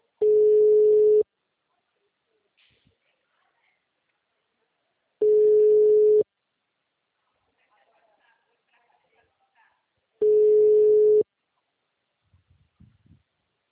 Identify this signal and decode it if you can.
Ringback tone (Europe, 425 Hz), with speech.